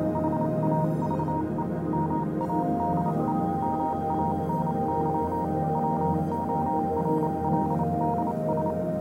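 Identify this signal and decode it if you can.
Morse code, heard as a keyed tone, about 28 wpm, spaced as for 25 wpm, 940 Hz, with noise. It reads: HAVE MADE MANY WORLDS
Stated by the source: Morse code version of a fan-made pony music track.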